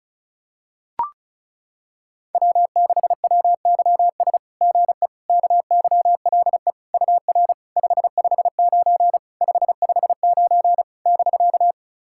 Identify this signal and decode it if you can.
Morse code, heard as a keyed tone, about 36 wpm, 690 Hz. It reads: W6WYS GE KYLE UR 559 559 <BK>
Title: SOTA QSO Practice 35wpm